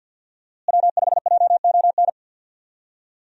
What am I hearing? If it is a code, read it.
Morse code, heard as a keyed tone, about 48 wpm, 700 Hz. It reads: W5JON
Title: USA Call Signs - 5 Characters - Repeated 2x - 50wpm